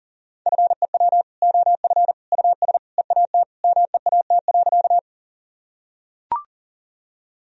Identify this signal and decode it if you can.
Morse code, heard as a keyed tone, about 39 wpm, 700 Hz. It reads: FEW OF US EAT MEAT.
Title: Sentences from Top 1000 Words - Review Part 3 - 40wpm